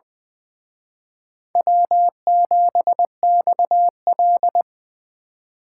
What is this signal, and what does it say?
Morse code, heard as a keyed tone, about 24 wpm, 700 Hz. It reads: W7XL
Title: Speed Racing - USA Call Signs 20wpm